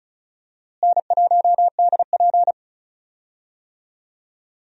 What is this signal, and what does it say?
Morse code, heard as a keyed tone, about 36 wpm, 700 Hz. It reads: N1DP
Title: USA Call Signs - 4 Characters - Repeated 2x - 35wpm